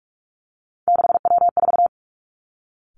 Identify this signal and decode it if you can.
Morse code, heard as a keyed tone, about 46 wpm, 700 Hz. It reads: 6W4
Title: Three Challenging Letter-Number Combinations 45wpm